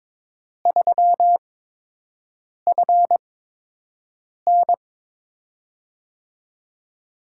Morse code, heard as a keyed tone, about 21 wpm, spaced as for 6 wpm, 700 Hz. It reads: 3FN
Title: USA Call Signs - 5 Characters - 8x Character Spacing - 22wpm